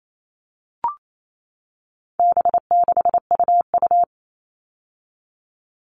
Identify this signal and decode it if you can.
Morse code, heard as a keyed tone, about 28 wpm, 700 Hz. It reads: B6UU
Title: All Cut Numbers - 4-Digit 28wpm